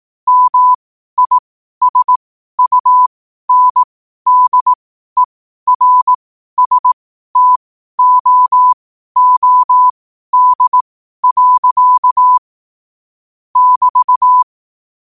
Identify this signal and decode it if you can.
Morse code, heard as a keyed tone, about 18 wpm, spaced as for 12 wpm, 1000 Hz. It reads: MISUNDERSTOOD. <BT>
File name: qotdfiles / 12wpm